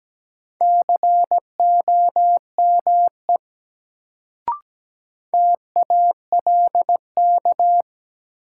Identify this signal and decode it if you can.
Morse code, heard as a keyed tone, about 17 wpm, 700 Hz. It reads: COME TALK
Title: Sets of 2 Words - New Word Drill - 17wpm